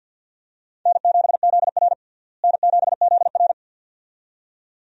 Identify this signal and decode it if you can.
Morse code, heard as a keyed tone, about 51 wpm, 700 Hz. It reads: N7ZR N7ZR
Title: USA Call Signs - 4 Characters - Repeated 2x - 50wpm